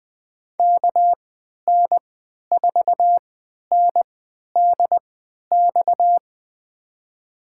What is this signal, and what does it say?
Morse code, heard as a keyed tone, about 21 wpm, spaced as for 11 wpm, 700 Hz. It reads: KN4NDX
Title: USA Call Signs - 3x Character Spacing - 20wpm